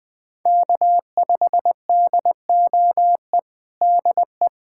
Morse code, heard as a keyed tone, about 20 wpm, 700 Hz. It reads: K5DOE DE
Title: USA Callsign DE Callsign - 5 Characters - 20wpm